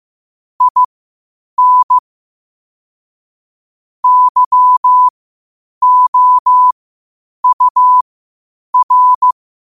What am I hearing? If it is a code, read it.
Morse code, heard as a keyed tone, about 15 wpm, spaced as for 8 wpm, 1000 Hz. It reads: IN YOUR
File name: qotdfiles / 8wpm